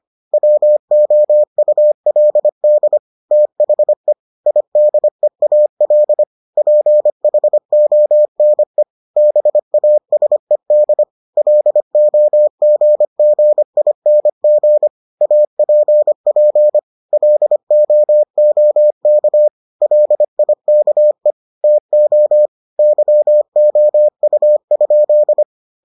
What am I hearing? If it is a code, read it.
Morse code, heard as a keyed tone, about 25 wpm, 600 Hz. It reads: WOULD THE IDEAL PHONE BASED LOGGING APP LOOK LIKE TO YOU?